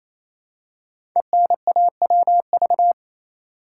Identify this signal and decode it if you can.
Morse code, heard as a keyed tone, about 27 wpm, 700 Hz. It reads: ENAWV
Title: Code Groups - 5 Letters - 28wpm